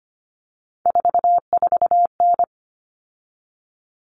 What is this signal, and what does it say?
Morse code, heard as a keyed tone, about 25 wpm, 700 Hz. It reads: 44N
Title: Common Cut Numbers - 3-Digit 25wpm